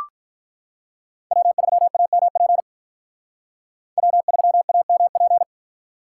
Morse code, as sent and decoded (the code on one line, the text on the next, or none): .-- ...-- .- -- .--. / .-- ...-- .- -- .--.
W3AMP W3AMP